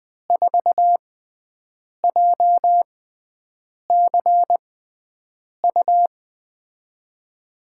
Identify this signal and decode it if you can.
Morse code, heard as a keyed tone, about 20 wpm, spaced as for 7 wpm, 700 Hz. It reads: 4JCU
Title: USA Call Signs - 5 Characters - 6x Character Spacing - 20wpm